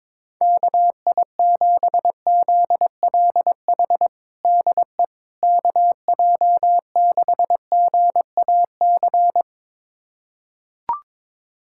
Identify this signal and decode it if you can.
Morse code, heard as a keyed tone, about 22 wpm, 700 Hz. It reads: KI7ZLH DE KJ6GAC
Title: USA Callsign DE Callsign - 6 Characters - 22wpm